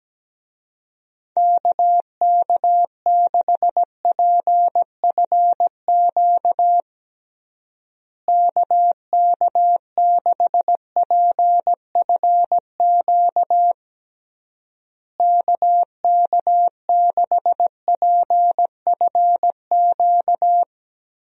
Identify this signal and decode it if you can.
Morse code, heard as a keyed tone, about 17 wpm, 700 Hz. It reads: KK6PFQ KK6PFQ KK6PFQ
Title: USA Call Signs - 6 Characters - Repeated 3x - 17wpm